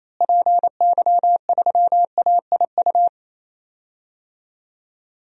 Morse code, heard as a keyed tone, about 28 wpm, 700 Hz. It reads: PY3AIU
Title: DXCC-100 Call Signs - 6 Characters - Repeated 3x - 28wpm